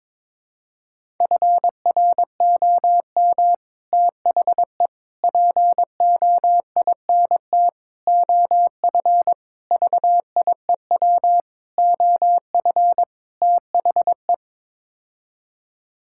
Morse code, heard as a keyed tone, about 22 wpm, 700 Hz. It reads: FROM THE POINT OF VIEW OF THE
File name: practice / 22wpm / Sets of 7 Words - 22wpm